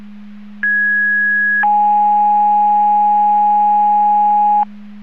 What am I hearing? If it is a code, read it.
Two-tone page, heard as a one-second tone, then a three-second tone, about 1650 then 830 Hz. It is heard from 0.6 to 4.6 s.
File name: Portage / 00079 11-01 Portage 154.385 Paging